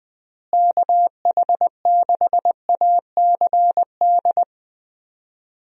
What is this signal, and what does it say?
Morse code, heard as a keyed tone, about 21 wpm, 700 Hz. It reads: KH6ACD
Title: USA Call Signs - 6 Characters - Repeated 3x - 20wpm